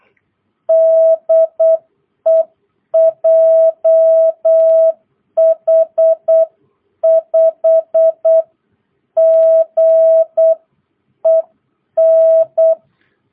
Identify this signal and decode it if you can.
Morse code, heard as a keyed tone, about 8 wpm, 650 Hz. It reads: DE JH5GEN